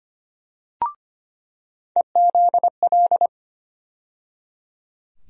Morse code, heard as a keyed tone, about 25 wpm, 700 Hz. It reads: EZL
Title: USA Call Sign Postfixes - 3 Letters 25wpm